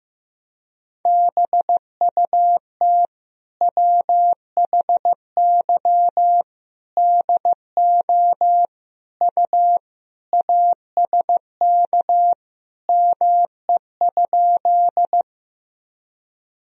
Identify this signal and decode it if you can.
Morse code, heard as a keyed tone, about 14 wpm, 700 Hz. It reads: BUT WHY DO U ASK ME?
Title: Sentences from Top 200 Words and Abbreviations 15wpm